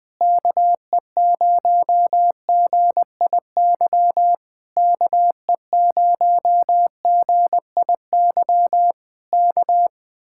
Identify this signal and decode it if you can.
Morse code, heard as a keyed tone, about 20 wpm, 700 Hz. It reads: KE0GIY KE0GIY K
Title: Speed-Racing Common QSO Phrases Encore - 20wpm